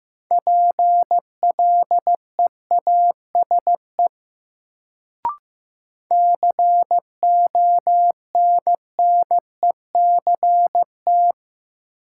Morse code, heard as a keyed tone, about 15 wpm, 700 Hz. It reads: PLEASE CONNECT
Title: Sentences from Top 1500 Words - New Word Drill Alt - 15wpm